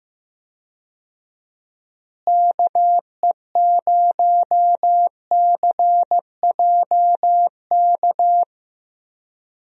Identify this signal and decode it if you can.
Morse code, heard as a keyed tone, about 15 wpm, 700 Hz. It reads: KE0CJK
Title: USA Call Signs - 6 Characters - Repeated 2x - 15wpm